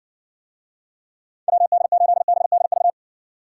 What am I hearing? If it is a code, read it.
Morse code, heard as a keyed tone, about 68 wpm, 700 Hz. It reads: WD8BDV